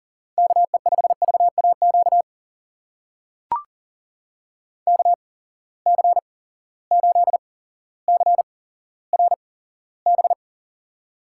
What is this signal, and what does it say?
Morse code, heard as a keyed tone, about 38 wpm, spaced as for 17 wpm, 690 Hz. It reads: KE5VAQ KC8CRB
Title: USA Call Signs - 8x Character Spacing - 40wpm